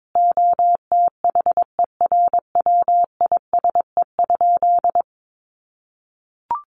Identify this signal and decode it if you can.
Morse code, heard as a keyed tone, about 22 wpm, 700 Hz. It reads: OTHERWISE?
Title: Sentences from Top 1200 Words 22wpm